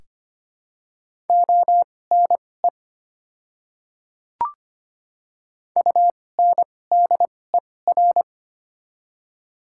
Morse code, heard as a keyed tone, about 25 wpm, spaced as for 18 wpm, 700 Hz. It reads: ONE UNDER
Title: 2x Character Spacing - Top 300 Words - 25wpm